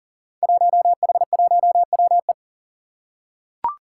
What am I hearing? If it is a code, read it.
Morse code, heard as a keyed tone, about 38 wpm, 700 Hz. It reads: 1H1WE
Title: Five Challenging Letter-Number Combinations 40wpm